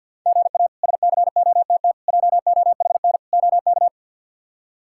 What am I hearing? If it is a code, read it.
Morse code, heard as a keyed tone, about 52 wpm, 700 Hz. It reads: GA SCOTT JOHN OK
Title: K1USN SST QSO Practice - 50wpm